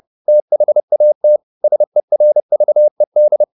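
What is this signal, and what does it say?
Morse code, heard as a keyed tone, about 30 wpm, 600 Hz. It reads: THAT SERVED
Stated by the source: Morse code at 30 wpm, the novel Never by Ken Follett in English, from CW med Gnister.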